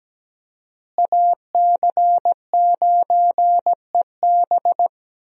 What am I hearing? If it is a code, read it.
Morse code, heard as a keyed tone, about 17 wpm, 700 Hz. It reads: AC9EB
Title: USA Call Signs - 5 Characters - Repeated 3x - 17wpm